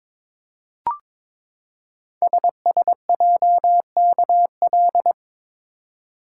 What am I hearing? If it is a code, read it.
Morse code, heard as a keyed tone, about 22 wpm, 690 Hz. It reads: SSJKL
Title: Code Groups - 5 Letters - 22wpm